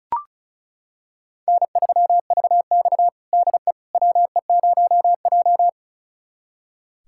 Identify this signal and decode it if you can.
Morse code, heard as a keyed tone, about 36 wpm, 700 Hz. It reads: N3VX DE WE0J
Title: USA Callsign DE Callsign - 4 Characters - 35wpm